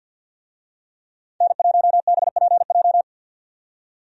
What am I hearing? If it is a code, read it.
Morse code, heard as a keyed tone, about 51 wpm, 700 Hz. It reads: N1BPJ
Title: USA Call Signs - 5 Characters - 50wpm